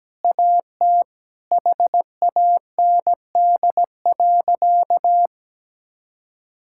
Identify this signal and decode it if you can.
Morse code, heard as a keyed tone, about 17 wpm, 700 Hz. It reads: AT HAND.
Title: Sentences from Top 1600 Words 17wpm